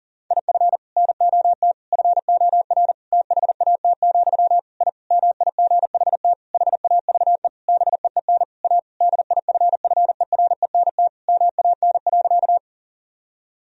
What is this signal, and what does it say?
Morse code, heard as a keyed tone, about 40 wpm, 700 Hz. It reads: IF NOT FOR THAT, I MIGHT HAVE BEEN A DIFFERENT MAN.
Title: Sentences from Top 1000 Words - Review Part 3 - 40wpm